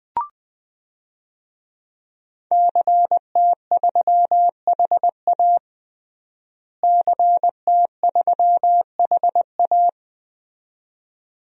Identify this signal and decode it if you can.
Morse code, heard as a keyed tone, about 21 wpm, 700 Hz. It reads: CT3HA CT3HA
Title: DXCC-100 Call Signs - 5 Characters - Repeated 2x - 20wpm